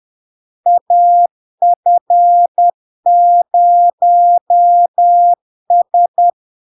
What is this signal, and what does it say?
Morse code, heard as a keyed tone, about 10 wpm, 700 Hz. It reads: AF0S
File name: LARK-FD-2022-Log-in-CW-10wpm-700-Hz